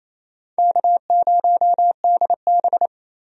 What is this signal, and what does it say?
Morse code, heard as a keyed tone, about 28 wpm, 700 Hz. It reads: K0DB
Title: USA Call Signs - 4 Characters - 28wpm